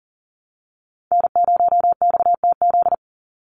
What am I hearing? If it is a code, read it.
Morse code, heard as a keyed tone, about 40 wpm, 700 Hz. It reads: N0XTZ
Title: USA Call Signs 40wpm